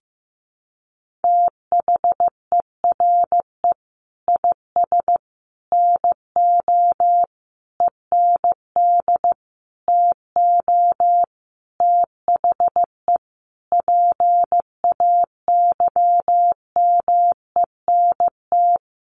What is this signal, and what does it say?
Morse code, heard as a keyed tone, about 15 wpm, 700 Hz. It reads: THERE IS NO END TO THE PAYMENT
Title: Sentences from Top 1200 Words 15wpm